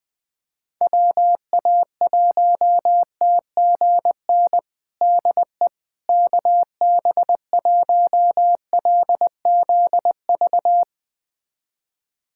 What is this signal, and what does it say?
Morse code, heard as a keyed tone, about 20 wpm, 700 Hz. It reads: WA1TGN DE KB1LZV
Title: USA Callsign DE Callsign - 6 Characters - 20wpm